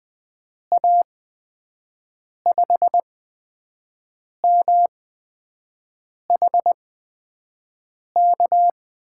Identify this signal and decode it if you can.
Morse code, heard as a keyed tone, about 20 wpm, spaced as for 5 wpm, 700 Hz. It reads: A5MHK
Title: USA Call Signs - 6 Characters - 8x Character Spacing - 20wpm